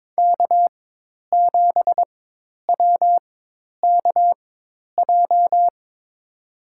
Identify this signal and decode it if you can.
Morse code, heard as a keyed tone, about 22 wpm, spaced as for 10 wpm, 700 Hz. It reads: K7WKJ
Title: USA Call Signs - 5 Characters - 4x Character Spacing - 22wpm